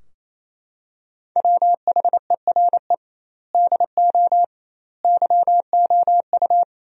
Morse code, heard as a keyed tone, about 27 wpm, 700 Hz. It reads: WHERE DO YOU
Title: 2x Word-Spacing - Sentences from Top 200 Words - 28wpm